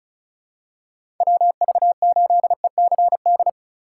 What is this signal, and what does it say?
Morse code, heard as a keyed tone, about 36 wpm, 700 Hz. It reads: WV8ECD
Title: USA Call Signs - 6 Characters - 8x Character Spacing - 35wpm